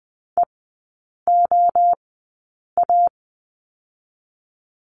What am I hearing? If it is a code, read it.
Morse code, heard as a keyed tone, about 20 wpm, 700 Hz. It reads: E O A
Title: 023 Lesson 5e - Letter o - Three Characters 20wpm